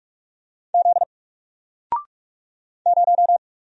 Morse code, heard as a keyed tone, about 45 wpm, 700 Hz. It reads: Z 0
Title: Single Character (Letter:Number:Punctuation:Prosign) 45wpm